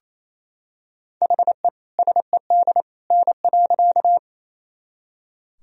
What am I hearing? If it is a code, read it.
Morse code, heard as a keyed tone, about 28 wpm, 700 Hz. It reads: HE SED N.
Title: Sentences from Top 100 Words and Abbreviations 28wpm